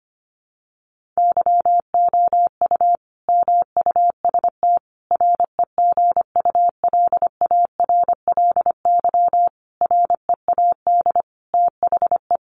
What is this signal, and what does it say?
Morse code, heard as a keyed tone, about 25 wpm, 700 Hz. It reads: YOU MUST REGULARLY READ THE
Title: Sentences from Top 1400 Words 25wpm